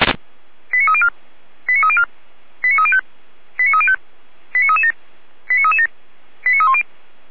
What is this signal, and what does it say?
Five-tone selective call (ZVEI), heard as quick runs of five tones, tones of about 70 ms. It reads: 89304 89304 89305 89305 89337 89337 89310
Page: SDR/ZVEI